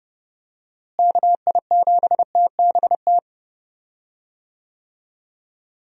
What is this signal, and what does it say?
Morse code, heard as a keyed tone, about 30 wpm, 700 Hz. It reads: KI7TBT
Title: USA Call Signs - 6 Characters - Repeated 3x - 30wpm